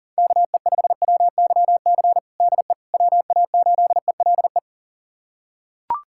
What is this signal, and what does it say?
Morse code, heard as a keyed tone, about 40 wpm, 700 Hz. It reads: KE5WYC DE WA8ELE
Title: USA Callsign DE Callsign - 6 Characters - 40wpm